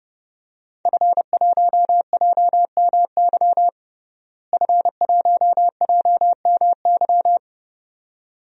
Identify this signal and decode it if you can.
Morse code, heard as a keyed tone, about 30 wpm, 700 Hz. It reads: F1JMY F1JMY
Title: DXCC-100 Call Signs - 5 Characters - Repeated 2x - 30wpm